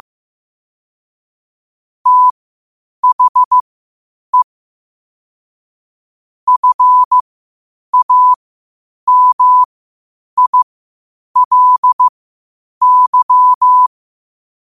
Morse code, heard as a keyed tone, about 15 wpm, spaced as for 8 wpm, 1000 Hz. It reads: THE FAMILY
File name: qotdfiles / 8wpm